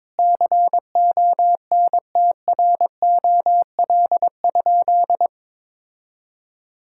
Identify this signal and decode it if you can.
Morse code, heard as a keyed tone, about 22 wpm, 700 Hz. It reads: CONTROL?
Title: Sentences from Top 1000 Words - Review Part 5 - 22wpm